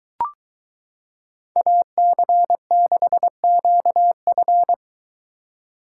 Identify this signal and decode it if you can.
Morse code, heard as a keyed tone, about 23 wpm, 700 Hz. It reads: AC6QF